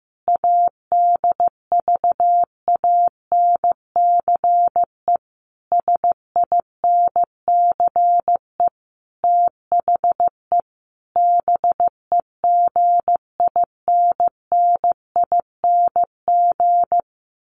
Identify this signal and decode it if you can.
Morse code, heard as a keyed tone, about 15 wpm, 700 Hz. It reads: ADVANCE SINCE THE BEGINNING
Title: Sentences from Top 1400 Words 15wpm